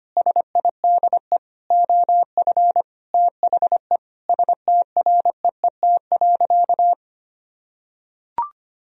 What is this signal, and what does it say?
Morse code, heard as a keyed tone, about 25 wpm, 700 Hz. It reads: SIDE OF THE STREET.
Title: Sentences from Top 1000 Words - Review Part 3 - 25wpm